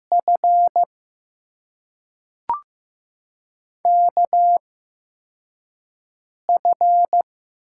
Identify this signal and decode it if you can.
Morse code, heard as a keyed tone, about 15 wpm, spaced as for 4 wpm, 700 Hz. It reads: F KF
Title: USA Call Signs - 6 Characters - 8x Character Spacing - 15wpm